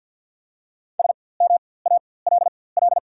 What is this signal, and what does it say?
Morse code, heard as a keyed tone, about 54 wpm, spaced as for 24 wpm, 700 Hz. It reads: SMALL
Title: 4x Character Spacing - Top 300 Words - 50wpm